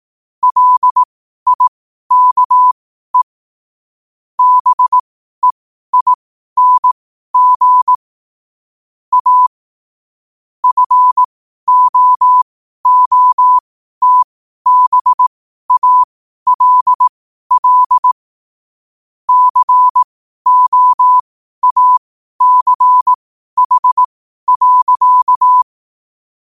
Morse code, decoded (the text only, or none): LIKE BEING A FOOTBALL COACH.